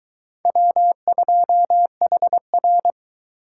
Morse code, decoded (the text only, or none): W2HR